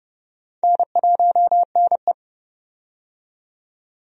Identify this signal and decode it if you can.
Morse code, heard as a keyed tone, about 32 wpm, 700 Hz. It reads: N1NE
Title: USA Call Signs - 4 Characters - 30wpm